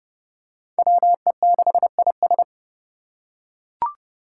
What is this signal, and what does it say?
Morse code, heard as a keyed tone, about 30 wpm, 700 Hz. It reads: WE6IS
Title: Five Challenging Letter-Number Combinations 30wpm